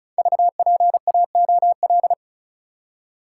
Morse code, heard as a keyed tone, about 36 wpm, 700 Hz. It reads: VPAOL
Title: Code Groups - 5 Letters - 35wpm